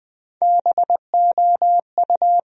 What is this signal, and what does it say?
Morse code, heard as a keyed tone, about 20 wpm, 700 Hz. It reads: BOU
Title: USA Call Sign Postfixes - 3 Letters 20wpm